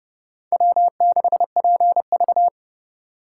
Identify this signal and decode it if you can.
Morse code, heard as a keyed tone, about 27 wpm, 700 Hz. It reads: W6PV